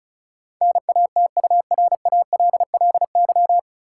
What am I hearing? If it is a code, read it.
Morse code, heard as a keyed tone, about 36 wpm, 700 Hz. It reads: NATURALLY
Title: Sentences from Top 1300 Words - New Word Drill - 35wpm